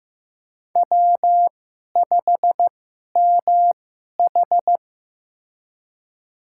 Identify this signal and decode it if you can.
Morse code, heard as a keyed tone, about 15 wpm, spaced as for 11 wpm, 700 Hz. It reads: W5MH
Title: USA Call Signs - 4 Characters - 2x Character Spacing - 15wpm